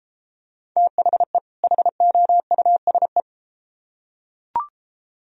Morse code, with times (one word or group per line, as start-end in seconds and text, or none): 0.8-1.4 THE
1.6-3.2 HOUSE